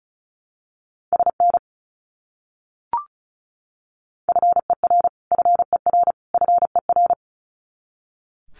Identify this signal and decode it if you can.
Morse code, heard as a keyed tone, about 34 wpm, 700 Hz. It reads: SN FER FER FER
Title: Abbreviations Repeated 3x - 35wpm